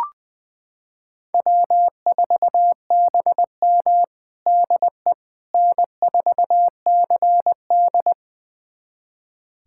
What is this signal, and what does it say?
Morse code, heard as a keyed tone, about 21 wpm, 700 Hz. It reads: W4BM DE N4CD
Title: USA Callsign DE Callsign - 4 Characters - 20wpm